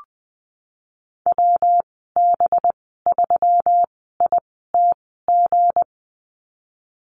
Morse code, heard as a keyed tone, about 19 wpm, spaced as for 14 wpm, 700 Hz. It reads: WB3ITG
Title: 231 Lesson 37g - Letter x - Call Signs 20wpm